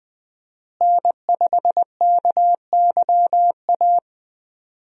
Morse code, heard as a keyed tone, about 20 wpm, 700 Hz. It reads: N5KYA
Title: USA Call Signs - 5 Characters - Repeated 2x - 20wpm